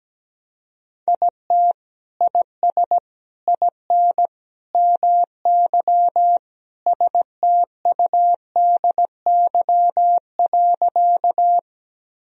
Morse code, heard as a keyed tone, about 17 wpm, 700 Hz. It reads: IT IS IN MY STUDY.
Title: Sentences from Top 1000 Words - Review Part 1 - 17wpm